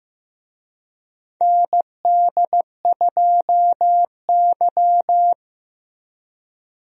Morse code, heard as a keyed tone, about 15 wpm, 700 Hz. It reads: ND2Y
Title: USA Call Signs - 4 Characters - Repeated 2x - 15wpm